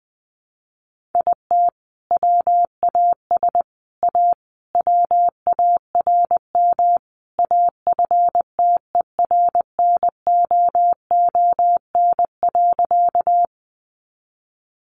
Morse code, heard as a keyed tone, about 20 wpm, 700 Hz. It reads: IT WAS A WARM AFTERNOON.
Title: Sentences from Top 1500 Words 20wpm